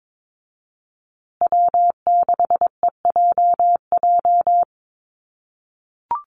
Morse code, heard as a keyed tone, about 22 wpm, 700 Hz. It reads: W6EJJ
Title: USA Call Signs Repeated 3x 22wpm